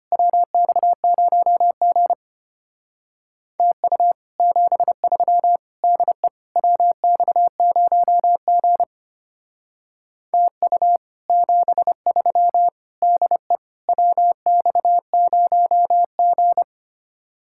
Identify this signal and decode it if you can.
Morse code, heard as a keyed tone, about 29 wpm, 700 Hz. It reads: WX0G TU 73 DE WX0G TU 73 DE WX0G
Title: Speed-Racing Common QSO Phrases - 25wpm